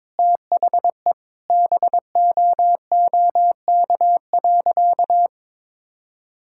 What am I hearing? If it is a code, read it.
Morse code, heard as a keyed tone, about 22 wpm, 700 Hz. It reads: THE BOOK.
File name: practice / 22wpm / Sentences with 2-Word Phrases - 22wpm